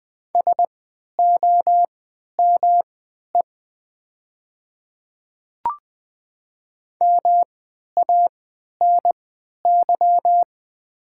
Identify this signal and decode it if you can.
Morse code, heard as a keyed tone, about 20 wpm, spaced as for 11 wpm, 700 Hz. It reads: SOME MANY
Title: 3x Character Spacing - Top 200 Words - 20wpm